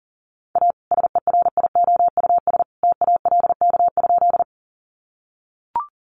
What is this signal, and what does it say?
Morse code, heard as a keyed tone, about 40 wpm, 700 Hz. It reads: A SERIOUS TALK?
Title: Sentences from Top 1100 Words 40wpm